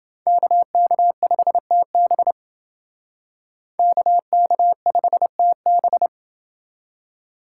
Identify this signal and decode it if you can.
Morse code, heard as a keyed tone, about 30 wpm, 700 Hz. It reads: KK5TB KK5TB